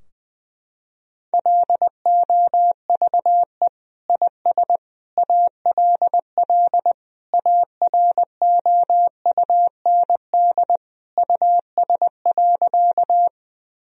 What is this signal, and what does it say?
Morse code, heard as a keyed tone, about 20 wpm, 700 Hz. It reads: LOVE IS ALL AROUND US.
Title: Sentences from Top 1000 Words - Review Part 3 - 20wpm